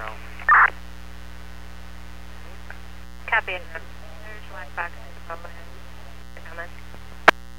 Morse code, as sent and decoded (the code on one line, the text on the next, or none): none